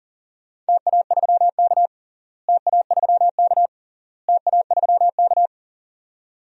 Morse code, heard as a keyed tone, about 40 wpm, 700 Hz. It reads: TA3K TA3K TA3K
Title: DXCC-100 Call Signs - 4 Characters - Repeated 3x - 40wpm